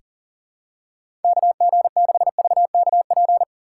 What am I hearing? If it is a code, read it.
Morse code, heard as a keyed tone, about 40 wpm, 700 Hz. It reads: KG6VKP
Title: USA Call Signs - 6 Characters - 3x Character Spacing - 40wpm